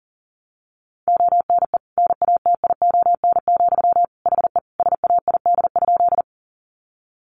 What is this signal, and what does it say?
Morse code, heard as a keyed tone, about 40 wpm, 700 Hz. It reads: ONE NATION, HE SAID?
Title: Sentences from Top 900 Words 40wpm